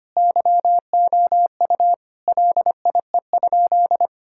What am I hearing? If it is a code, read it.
Morse code, heard as a keyed tone, about 25 wpm, 700 Hz. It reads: YOU LIE?
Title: Sentences from Top 1600 Words 25wpm